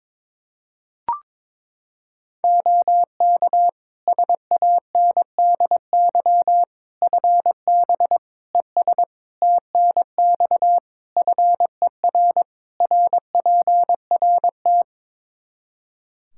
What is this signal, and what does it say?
Morse code, heard as a keyed tone, about 22 wpm, 700 Hz. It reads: OK SANDY FB ES TNX FER RPRT
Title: K1USN SST QSO Practice 22wpm